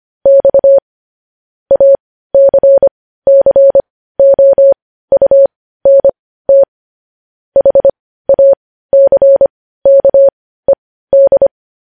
Morse code, heard as a keyed tone, about 25 wpm, spaced as for 15 wpm, 550 Hz. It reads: X ACCOUNT HACKED
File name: podcast / News headlines-1x-25x15